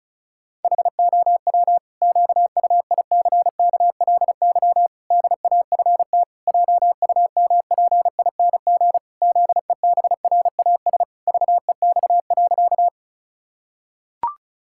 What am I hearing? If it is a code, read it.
Morse code, heard as a keyed tone, about 36 wpm, 700 Hz. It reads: HOW QUICKLY DAFT JUMPING ZEBRAS VEX.
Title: Pangram Challenge v2 - 35wpm